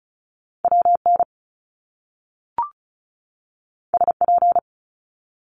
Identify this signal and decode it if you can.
Morse code, heard as a keyed tone, about 36 wpm, 700 Hz. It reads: W N S P
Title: Top Two Letter Combinations in Words 35wpm